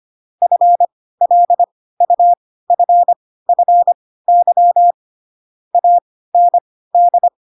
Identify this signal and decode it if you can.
Morse code, heard as a keyed tone, about 25 wpm, spaced as for 16 wpm, 700 Hz. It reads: FLUFFY AND